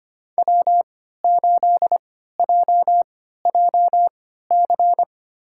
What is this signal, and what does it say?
Morse code, heard as a keyed tone, about 25 wpm, spaced as for 14 wpm, 700 Hz. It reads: W8JJC
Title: USA Call Signs - 5 Characters - 3x Character Spacing - 25wpm